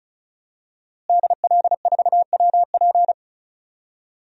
Morse code, heard as a keyed tone, about 36 wpm, 700 Hz. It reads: DL4WP
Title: DXCC-100 Call Signs - 5 Characters - Repeated 2x - 35wpm